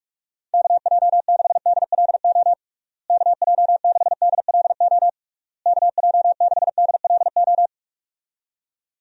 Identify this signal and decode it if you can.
Morse code, heard as a keyed tone, about 42 wpm, 700 Hz. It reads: KJ6DLO KJ6DLO KJ6DLO
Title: USA Call Signs - 6 Characters - Repeated 3x - 45wpm